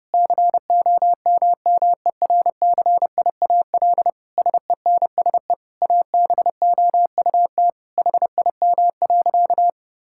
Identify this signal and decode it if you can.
Morse code, heard as a keyed tone, about 32 wpm, 700 Hz. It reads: COMMERCIAL SENSE ABOUT HIM.
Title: Sentences from Top 1000 Words - Review Part 5 - 30wpm